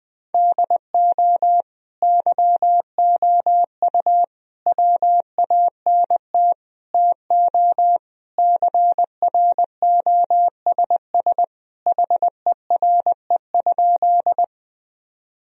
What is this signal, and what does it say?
Morse code, heard as a keyed tone, about 21 wpm, 700 Hz. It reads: DO YOU WANT TO CROSS HERE?
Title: Sentences from Top 1000 Words - Review Part 1 - 20wpm